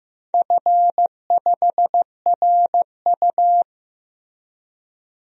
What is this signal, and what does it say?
Morse code, heard as a keyed tone, about 15 wpm, 700 Hz. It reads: F5RU
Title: DXCC-100 Call Signs - 4 Characters - Repeated 3x - 15wpm